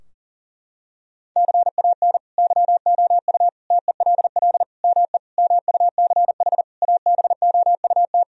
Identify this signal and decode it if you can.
Morse code, heard as a keyed tone, about 40 wpm, 700 Hz. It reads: CAN YOU TELL ME MUCH ABOUT
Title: Sentences from Top 1000 Words - Review Part 4 - 40wpm